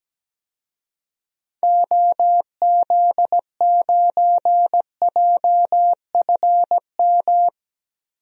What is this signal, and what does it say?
Morse code, heard as a keyed tone, about 17 wpm, 700 Hz. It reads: OZ9JFM
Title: DXCC-100 Call Signs - 6 Characters - Repeated 2x - 17wpm